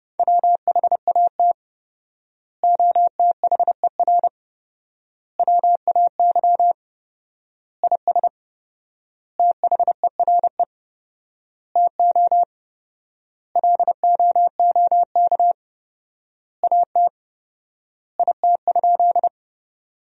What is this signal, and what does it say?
Morse code, heard as a keyed tone, about 30 wpm, 700 Hz. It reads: WHAT OTHER WAY IS THERE TO LOOK AT IT?
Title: 4x Word-Spacing - Sentences from Top 100 Words - 30wpm